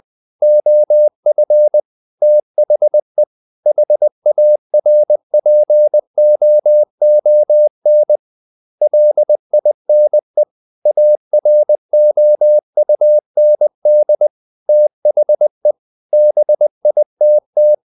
Morse code, heard as a keyed tone, about 20 wpm, 600 Hz. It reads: OF THE HARPOON LINE AROUND THE BITT
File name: the old man and the sea part 2 20wpm